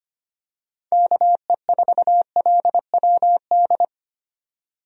Morse code, heard as a keyed tone, about 25 wpm, 700 Hz. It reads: KE4LWD
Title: USA Call Signs - 6 Characters - 25wpm